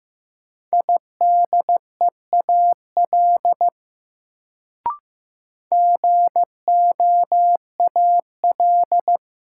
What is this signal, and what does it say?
Morse code, heard as a keyed tone, about 15 wpm, 700 Hz. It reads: IDEAL GOAL
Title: Sentences from Top 1400 Words - New Word Drill Alt - 15wpm